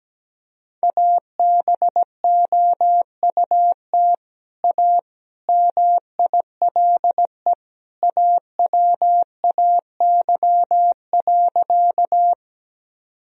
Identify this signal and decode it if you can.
Morse code, heard as a keyed tone, about 17 wpm, 700 Hz. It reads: ABOUT A MILE AWAY.
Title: Sentences from Top 1000 Words - Review Part 1 - 17wpm